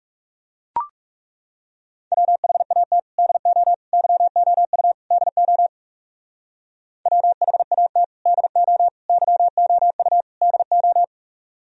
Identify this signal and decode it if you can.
Morse code, heard as a keyed tone, about 38 wpm, 700 Hz. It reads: WHAT DO YOU DO WHAT DO YOU DO